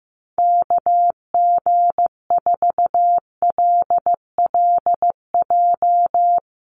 Morse code, heard as a keyed tone, about 15 wpm, 700 Hz. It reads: KG4LLJ
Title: USA Call Signs 15wpm